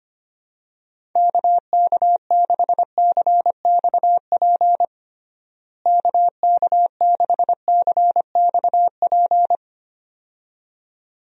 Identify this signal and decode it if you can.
Morse code, heard as a keyed tone, about 25 wpm, 700 Hz. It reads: KK6CXP KK6CXP
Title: USA Call Signs - 6 Characters - Repeated 2x - 25wpm